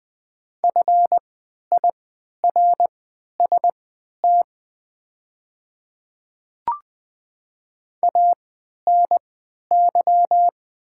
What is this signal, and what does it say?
Morse code, heard as a keyed tone, about 20 wpm, spaced as for 11 wpm, 700 Hz. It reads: FIRST ANY
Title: 3x Character Spacing - Top 200 Words - 20wpm